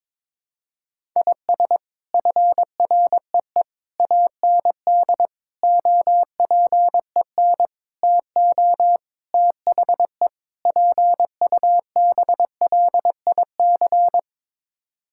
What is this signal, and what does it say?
Morse code, heard as a keyed tone, about 22 wpm, 700 Hz. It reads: IS FREE AND OPEN TO THE PUBLIC
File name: practice / 22wpm / Sets of 7 Words - 22wpm